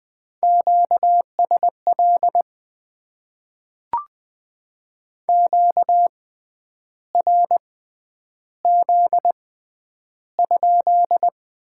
Morse code, heard as a keyed tone, about 20 wpm, spaced as for 9 wpm, 700 Hz. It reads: QSL QRZ?
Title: Top 90 QSO Elements - 6x Character Spacing - 20wpm